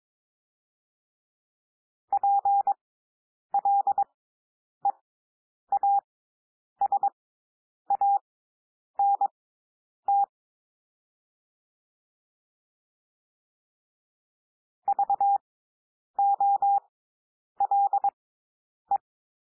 Morse code, heard as a keyed tone, about 22 wpm, spaced as for 9 wpm, 800 Hz. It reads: PLEASANT VOLE
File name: SOUND WORDS 22 wpm